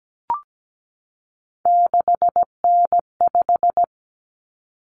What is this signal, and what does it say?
Morse code, heard as a keyed tone, about 17 wpm, 700 Hz. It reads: 6N5
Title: Common Cut Numbers - 3-Digit 17wpm